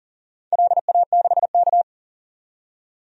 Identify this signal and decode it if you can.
Morse code, heard as a keyed tone, about 40 wpm, 700 Hz. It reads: LA6K
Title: DXCC-100 Call Signs - 4 Characters - 40wpm